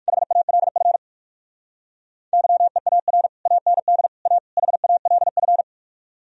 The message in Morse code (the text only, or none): HALF YEAR AND A HALF